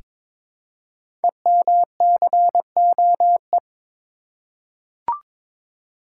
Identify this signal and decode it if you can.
Morse code, heard as a keyed tone, about 22 wpm, 700 Hz. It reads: EMCOE
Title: Code Groups - 5 Letter:Numbers - 22wpm